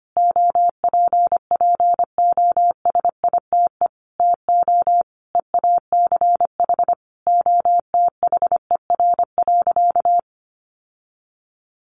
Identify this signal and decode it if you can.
Morse code, heard as a keyed tone, about 25 wpm, 700 Hz. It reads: OPPOSITE TO EACH OTHER.
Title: Sentences from Top 1300 Words 25wpm